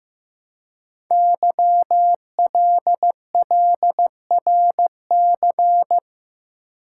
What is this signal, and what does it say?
Morse code, heard as a keyed tone, about 15 wpm, 700 Hz. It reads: YLLRC
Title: Code Groups - 5 Letters - 15wpm